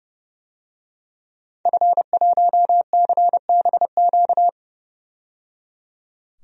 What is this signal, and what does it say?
Morse code, heard as a keyed tone, about 31 wpm, 700 Hz. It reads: F1CBQ
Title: DXCC-100 Call Signs - 5 Characters - 30wpm